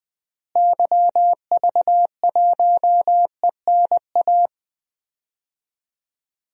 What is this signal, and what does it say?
Morse code, heard as a keyed tone, about 20 wpm, 700 Hz. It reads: YV1ENA
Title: DXCC-100 Call Signs - 6 Characters - Repeated 2x - 20wpm